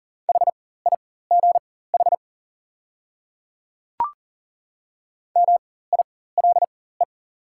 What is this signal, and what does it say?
Morse code, heard as a keyed tone, about 40 wpm, spaced as for 19 wpm, 700 Hz. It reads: HIGH MILE
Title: 4x Character Spacing - Top 500 Words - 40wpm